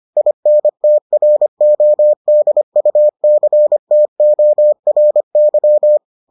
Morse code, heard as a keyed tone, about 25 wpm, 600 Hz. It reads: INTRODUCTORY